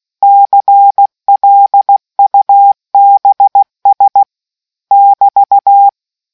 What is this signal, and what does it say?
Morse code, heard as a keyed tone, about 16 wpm, 800 Hz. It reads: CLUBS <BT>